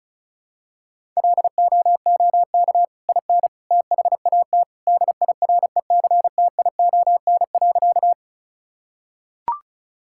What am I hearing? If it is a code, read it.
Morse code, heard as a keyed tone, about 36 wpm, 700 Hz. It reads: LOOK IN THAT DIRECTION.
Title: Sentences from Top 1000 Words - Review Part 5 - 35wpm